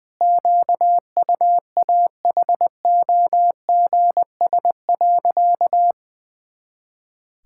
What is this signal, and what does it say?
Morse code, heard as a keyed tone, about 21 wpm, 700 Hz. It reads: QUAHOGS.
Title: Pangram Challenge v2 - 20wpm